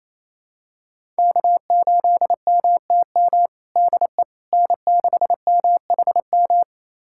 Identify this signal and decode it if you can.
Morse code, heard as a keyed tone, about 28 wpm, 700 Hz. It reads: K8MTM DE N6MHM
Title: USA Callsign DE Callsign - 5 Characters - 28wpm